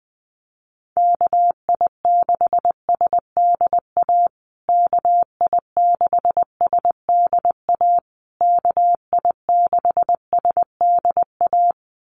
Morse code, heard as a keyed tone, about 20 wpm, 700 Hz. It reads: KI6SDA KI6SDA KI6SDA
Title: USA Call Signs Repeated 3x 20wpm